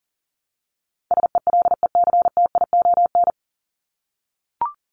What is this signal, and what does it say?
Morse code, heard as a keyed tone, about 40 wpm, 700 Hz. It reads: SELECTION
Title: Top 1251-1500 Words 40wpm